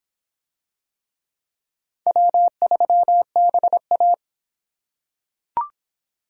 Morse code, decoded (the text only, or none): W3BA